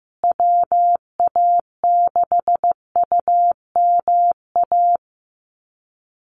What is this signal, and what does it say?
Morse code, heard as a keyed tone, about 15 wpm, 700 Hz. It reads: WA6UMA
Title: USA Call Signs Repeated 3x 15wpm